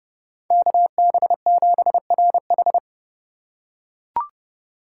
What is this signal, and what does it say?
Morse code, heard as a keyed tone, about 30 wpm, 700 Hz. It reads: KB7RH
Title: USA Call Signs - 2x Character Spacing - 30wpm